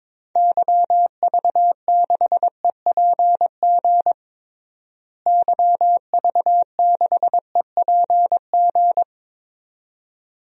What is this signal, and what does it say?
Morse code, heard as a keyed tone, about 22 wpm, 700 Hz. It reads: YV6EPG YV6EPG
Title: DXCC-100 Call Signs - 6 Characters - Repeated 2x - 22wpm